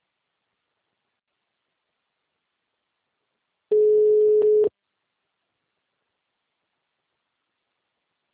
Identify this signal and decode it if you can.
Ringback tone (Europe, 425 Hz).
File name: out- -2004- - - .67